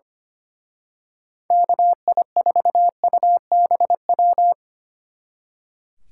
Morse code, heard as a keyed tone, about 29 wpm, 700 Hz. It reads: KI4UBW
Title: Speed Racing - USA 2x3 Call Signs 25wpm